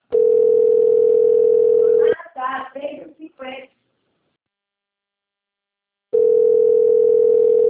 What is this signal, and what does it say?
Ringback tone (North America), 440 + 480 Hz.